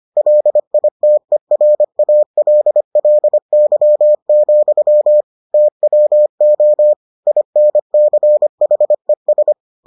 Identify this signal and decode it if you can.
Morse code, heard as a keyed tone, about 25 wpm, 600 Hz. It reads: LITERALLY, TWO INCHES